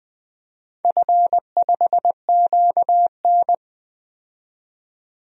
Morse code, heard as a keyed tone, about 20 wpm, 700 Hz. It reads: F5QN
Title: DXCC-100 Call Signs - 4 Characters - 20wpm